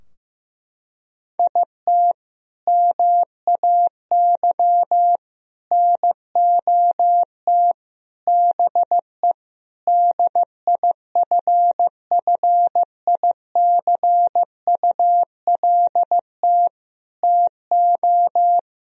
Morse code, heard as a keyed tone, about 15 wpm, 700 Hz. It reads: IT MAY NOT BE DIFFICULT TO
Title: Sentences from Top 1000 Words - Review Part 4 - 15wpm